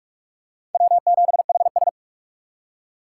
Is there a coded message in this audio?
Yes: Morse code from 0.7 to 1.9 s, heard as a keyed tone, about 40 wpm, 700 Hz. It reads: W7HS